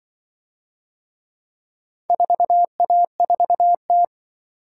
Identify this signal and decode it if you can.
Morse code, heard as a keyed tone, about 25 wpm, 700 Hz. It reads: 4A4T
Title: Speed-Racing - Four Letter-Number Combinations - 20wpm